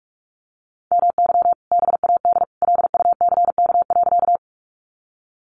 Morse code, heard as a keyed tone, about 44 wpm, 700 Hz. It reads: MY BAD LUCK.
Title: Sentences from Top 1500 Words 45wpm